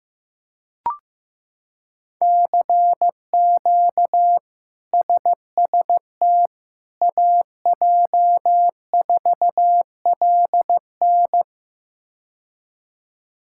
Morse code, heard as a keyed tone, about 15 wpm, 700 Hz. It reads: CQ SST AJ4LN
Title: K1USN SST QSO Practice - 15wpm